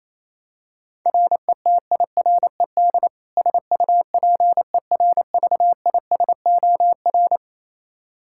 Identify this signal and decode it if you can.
Morse code, heard as a keyed tone, about 28 wpm, 700 Hz. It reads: RETIRED SUPERVISOR
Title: Practice QSO - Exchange 3 - More Info - 28wpm